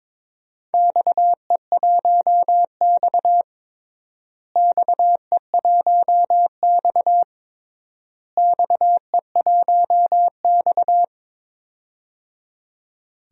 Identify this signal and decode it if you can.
Morse code, heard as a keyed tone, about 22 wpm, 700 Hz. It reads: XE1X XE1X XE1X
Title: DXCC-100 Call Signs - 4 Characters - Repeated 3x - 22wpm